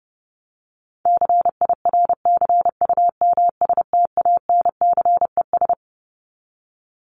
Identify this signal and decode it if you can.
Morse code, heard as a keyed tone, about 30 wpm, 700 Hz. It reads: CIRCUMSTANCES
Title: ES:S Suffix - Top 100 Words with Base Words 30wpm